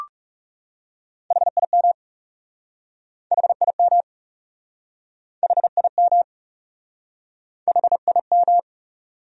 Morse code, heard as a keyed tone, about 40 wpm, 700 Hz. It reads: HIM HIM HIM HIM